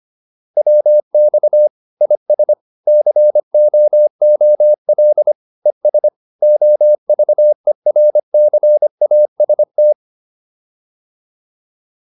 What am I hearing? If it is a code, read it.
Morse code, heard as a keyed tone, about 25 wpm, 600 Hz. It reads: WX IS COOL ES OVERCAST
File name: QSO406 25